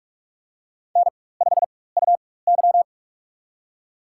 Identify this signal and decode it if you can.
Morse code, heard as a keyed tone, about 42 wpm, spaced as for 20 wpm, 700 Hz. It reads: N5UY
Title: USA Call Signs - 4 Characters - 4x Character Spacing - 45wpm